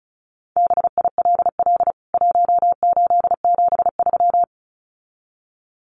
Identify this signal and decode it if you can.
Morse code, heard as a keyed tone, about 34 wpm, 700 Hz. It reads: BILL 1873
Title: CWT Exchange 35wpm